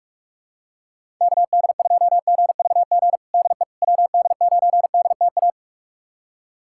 Morse code, heard as a keyed tone, about 47 wpm, 700 Hz. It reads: KD2GVG DE WD9DTA
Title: USA Callsign DE Callsign - 6 Characters - 45wpm